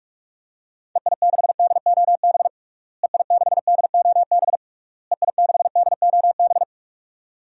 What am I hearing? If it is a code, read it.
Morse code, heard as a keyed tone, about 41 wpm, 700 Hz. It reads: EI6DOB EI6DOB EI6DOB
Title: DXCC-100 Call Signs - 6 Characters - Repeated 3x - 45wpm